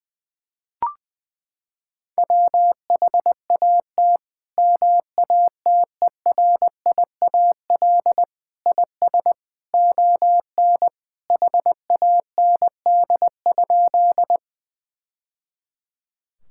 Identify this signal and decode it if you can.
Morse code, heard as a keyed tone, about 20 wpm, 700 Hz. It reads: WHAT MATERIAL IS ON HAND?
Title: Sentences from Top 1000 Words - Review Part 5 - 20wpm